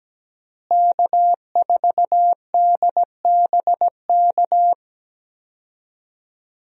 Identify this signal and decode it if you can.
Morse code, heard as a keyed tone, about 17 wpm, 700 Hz. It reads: K4DBK
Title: USA Call Signs - 5 Characters - Repeated 2x - 17wpm